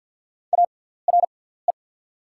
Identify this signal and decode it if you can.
Morse code, heard as a keyed tone, about 48 wpm, spaced as for 17 wpm, 690 Hz. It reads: ARE
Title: 6x Character Spacing - Top 100 Words - 50wpm